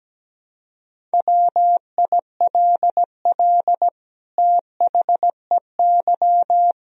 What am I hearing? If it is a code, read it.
Morse code, heard as a keyed tone, about 17 wpm, 700 Hz. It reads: WILL THEY
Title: Sentences from Top 1000 Words - Review Part 2 - 17wpm